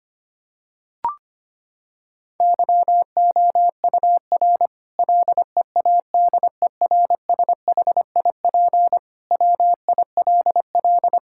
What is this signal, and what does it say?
Morse code, heard as a keyed tone, about 25 wpm, 700 Hz. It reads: YOUR LEADERSHIP WILL
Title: Sentences from Top 1600 Words 25wpm